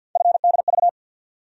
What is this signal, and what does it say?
Morse code, heard as a keyed tone, about 51 wpm, 700 Hz. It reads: FDV
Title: USA Call Sign Postfixes - 3 Letters 50wpm